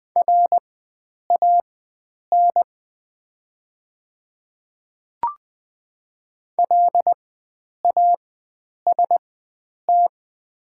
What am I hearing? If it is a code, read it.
Morse code, heard as a keyed tone, about 20 wpm, spaced as for 9 wpm, 700 Hz. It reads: RAN LAST
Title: 4x Character Spacing - Top 500 Words - 20wpm